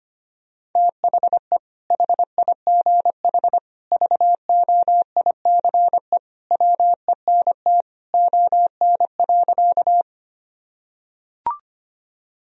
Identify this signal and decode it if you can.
Morse code, heard as a keyed tone, about 25 wpm, 700 Hz. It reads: THE HIGH VOICE WENT ON.
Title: Sentences from Top 1000 Words - Review Part 1 - 25wpm